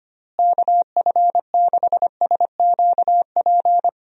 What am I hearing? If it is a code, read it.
Morse code, heard as a keyed tone, about 25 wpm, 700 Hz. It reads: KF6SQP
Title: USA Call Signs - 6 Characters - Repeated 2x - 25wpm